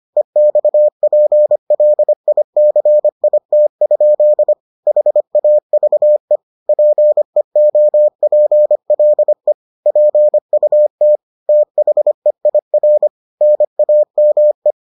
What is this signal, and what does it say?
Morse code, heard as a keyed tone, about 25 wpm, 600 Hz. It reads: EXPLICIT? HAVE PEOPLE PUT THEIR NAME